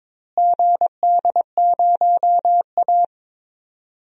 Morse code, heard as a keyed tone, about 22 wpm, 700 Hz. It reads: GD0A
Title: DXCC-100 Call Signs - 4 Characters - Repeated 2x - 22wpm